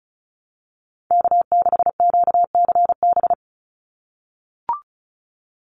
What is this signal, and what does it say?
Morse code, heard as a keyed tone, about 34 wpm, 700 Hz. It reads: K6QCB
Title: NA SOTA Chaser Callsigns 35wpm